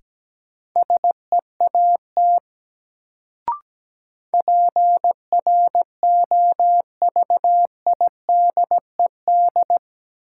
Morse code, heard as a keyed tone, about 17 wpm, 700 Hz. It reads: SEAT PROVIDED
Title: Sentences from Top 1500 Words - New Word Drill - 17wpm